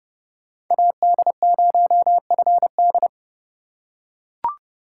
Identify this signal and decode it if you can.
Morse code, heard as a keyed tone, about 30 wpm, 700 Hz. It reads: AD0FD
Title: USA Call Signs - 5 Characters - 4x Character Spacing - 30wpm